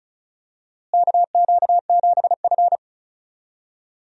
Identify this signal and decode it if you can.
Morse code, heard as a keyed tone, about 36 wpm, 700 Hz. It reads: KQ7F
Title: USA Call Signs - 4 Characters - Repeated 2x - 35wpm